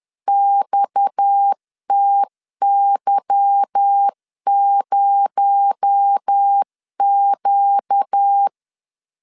Morse code, heard as a keyed tone, about 10 wpm, 800 Hz. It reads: XTY0Q
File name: LEZIONI / 9 286 10wpm 2735